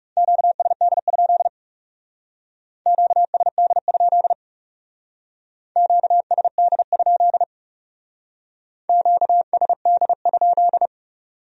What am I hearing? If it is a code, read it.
Morse code, heard as a keyed tone, about 36 wpm, 700 Hz. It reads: QSD? QSD? QSD? QSD?
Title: Speed-Racing Q-Codes - 30wpm